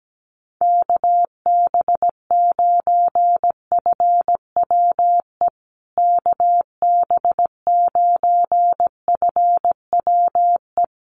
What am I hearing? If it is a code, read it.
Morse code, heard as a keyed tone, about 17 wpm, 700 Hz. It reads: KB9FWE KB9FWE
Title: USA Call Signs Repeated 2x 17wpm